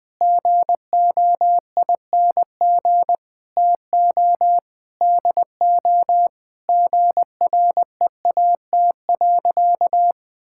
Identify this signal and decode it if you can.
Morse code, heard as a keyed tone, about 20 wpm, 700 Hz. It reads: GOING TO DO GREAT.
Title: Sentences with 4-Word Phrases - 20wpm